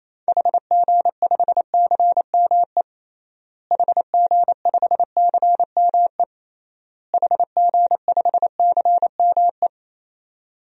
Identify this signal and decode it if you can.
Morse code, heard as a keyed tone, about 28 wpm, 700 Hz. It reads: HG5CME HG5CME HG5CME
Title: DXCC-100 Call Signs - 6 Characters - Repeated 3x - 28wpm